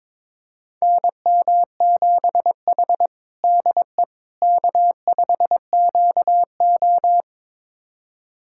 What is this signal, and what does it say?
Morse code, heard as a keyed tone, about 22 wpm, 700 Hz. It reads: NM7H DE K5QO
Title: USA Callsign DE Callsign - 4 Characters - 22wpm